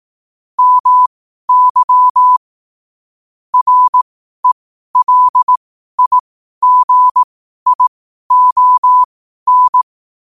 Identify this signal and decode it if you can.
Morse code, heard as a keyed tone, about 18 wpm, spaced as for 12 wpm, 1000 Hz. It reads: MY RELIGION
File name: 12wpm